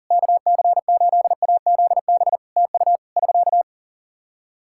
Morse code, heard as a keyed tone, about 40 wpm, 700 Hz. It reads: KC8AZB TU <SK>
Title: Practice QSO - Exchange 4 - Ending - 40wpm